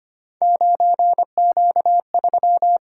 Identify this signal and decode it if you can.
Morse code, heard as a keyed tone, about 25 wpm, 700 Hz. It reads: 9Q3
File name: Three Letter-Number Combinations - 25wpm